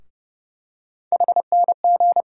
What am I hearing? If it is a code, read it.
Morse code, heard as a keyed tone, about 30 wpm, 700 Hz. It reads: HNG
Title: USA Call Sign Postfixes - 3 Letters 30wpm